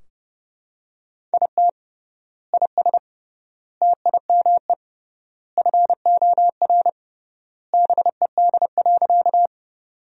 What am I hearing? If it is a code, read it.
Morse code, heard as a keyed tone, about 30 wpm, 700 Hz. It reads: IT IS TIME FOR BED.
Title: 3x Word-Spacing - Sentences from Top 500 Words - 30wpm